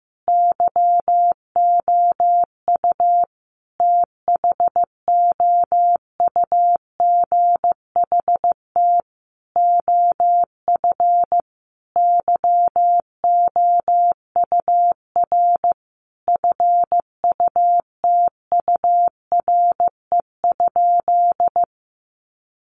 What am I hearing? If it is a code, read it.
Morse code, heard as a keyed tone, about 15 wpm, 700 Hz. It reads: YOU THOUGHT OF YOUR FUTURE?
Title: Sentences from Top 700 Words 15wpm